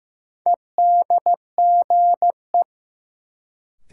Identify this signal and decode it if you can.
Morse code, heard as a keyed tone, about 15 wpm, 700 Hz. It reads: EDGE
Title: Sets of 3 Words - New Word Drill - 15wpm